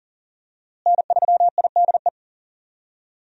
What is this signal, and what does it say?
Morse code, heard as a keyed tone, about 40 wpm, 700 Hz. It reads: N3IDE
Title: USA Call Signs - 5 Characters - Repeated 2x - 40wpm